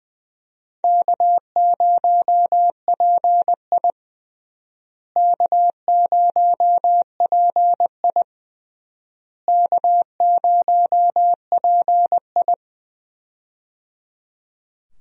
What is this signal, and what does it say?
Morse code, heard as a keyed tone, about 20 wpm, 700 Hz. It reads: K0PI K0PI K0PI
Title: USA Call Signs - 4 Characters - Repeated 3x - 20wpm